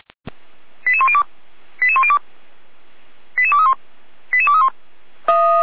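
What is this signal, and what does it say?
Five-tone selective call (ZVEI), heard as quick runs of five tones, tones of about 70 ms. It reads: 88102 88102 88231 88231
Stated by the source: Received by SDR, ZVEI five-tone selective call.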